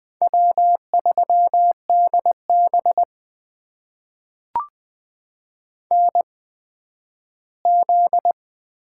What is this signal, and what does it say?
Morse code, heard as a keyed tone, about 20 wpm, spaced as for 7 wpm, 700 Hz. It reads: W3DBNZ
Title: USA Call Signs - 8x Character Spacing - 20wpm